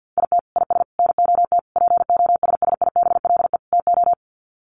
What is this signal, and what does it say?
Morse code, heard as a keyed tone, about 50 wpm, 700 Hz. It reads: IT IS NOT POSSIBLE TO
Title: Sets of 5 Words Encore2 50wpm